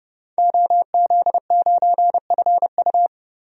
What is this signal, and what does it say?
Morse code, heard as a keyed tone, about 30 wpm, 700 Hz. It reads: OZ9FU
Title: DXCC-100 Call Signs - 5 Characters - 30wpm